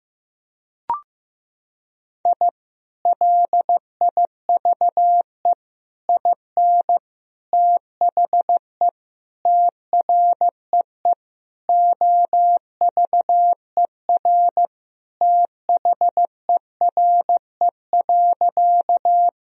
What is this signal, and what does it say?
Morse code, heard as a keyed tone, about 15 wpm, 700 Hz. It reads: I LIVE IN THE TREE OVER THERE.
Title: Sentences from Top 1000 Words - Review Part 1 - 15wpm